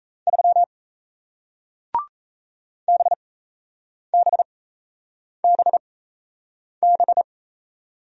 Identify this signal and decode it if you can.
Morse code, heard as a keyed tone, about 38 wpm, spaced as for 8 wpm, 700 Hz. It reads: 3 BBBB